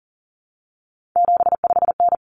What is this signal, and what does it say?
Morse code, heard as a keyed tone, about 40 wpm, 700 Hz. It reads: 75N
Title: Common Cut Numbers - 3-Digit 40wpm